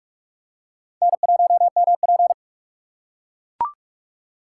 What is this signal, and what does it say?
Morse code, heard as a keyed tone, about 41 wpm, 700 Hz. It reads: N1MP